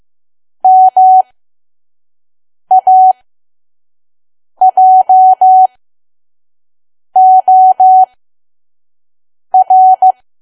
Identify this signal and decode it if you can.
Morse code, heard as a keyed tone, about 15 wpm, spaced as for 5 wpm, 750 Hz. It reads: MAJOR
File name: Archive / 5 WPM files / PM